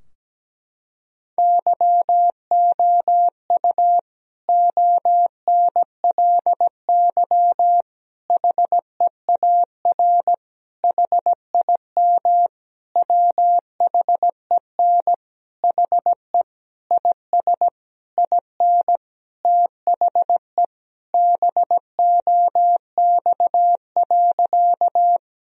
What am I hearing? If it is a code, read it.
Morse code, heard as a keyed tone, about 17 wpm, 700 Hz. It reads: YOU ONLY HEAR HIM WHEN HE IS IN THE BOX.
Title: Sentences from Top 1000 Words - Review Part 2 - 17wpm